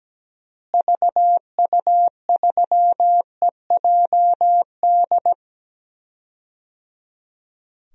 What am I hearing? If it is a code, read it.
Morse code, heard as a keyed tone, about 17 wpm, 700 Hz. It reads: VU3EJD
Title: DXCC-100 Call Signs - 6 Characters - Repeated 3x - 17wpm